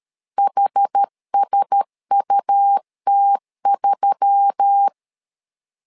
Morse code, heard as a keyed tone, about 13 wpm, 800 Hz. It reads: HSUT3
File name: LEZIONI / 5 LBVC / 5 L 12wpm 1913